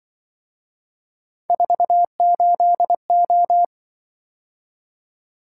Morse code, heard as a keyed tone, about 24 wpm, 700 Hz. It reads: 48O
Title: Speed-Racing - Three Letter-Number Combinations - 20wpm